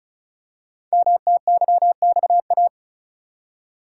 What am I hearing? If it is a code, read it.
Morse code, heard as a keyed tone, about 36 wpm, 690 Hz. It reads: MTYXA
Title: Code Groups - 5 Letters - 35wpm